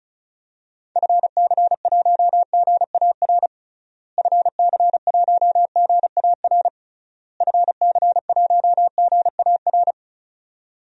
Morse code, heard as a keyed tone, about 34 wpm, 700 Hz. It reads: FC1GAR FC1GAR FC1GAR
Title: DXCC-100 Call Signs - 6 Characters - Repeated 3x - 35wpm